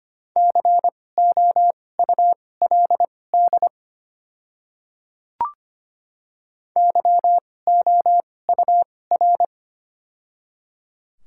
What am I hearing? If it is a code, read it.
Morse code, heard as a keyed tone, about 25 wpm, spaced as for 18 wpm, 700 Hz. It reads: COULD YOUR
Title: 2x Character Spacing - Top 200 Words - 25wpm